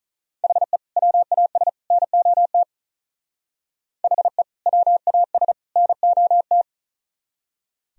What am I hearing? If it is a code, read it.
Morse code, heard as a keyed tone, about 44 wpm, 700 Hz. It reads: HE WAS NOT HE WAS NOT
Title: Speed-Racing - Sets of 3 Words 35wpm